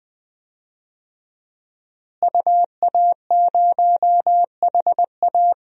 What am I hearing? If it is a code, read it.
Morse code, heard as a keyed tone, about 20 wpm, 700 Hz. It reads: UA0HA
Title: DXCC-100 Call Signs - 5 Characters - Repeated 3x - 20wpm